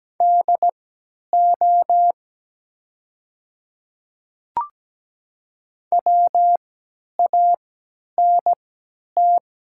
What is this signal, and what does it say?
Morse code, heard as a keyed tone, about 17 wpm, spaced as for 10 wpm, 700 Hz. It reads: DO WANT
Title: 3x Character Spacing - Top 200 Words - 17wpm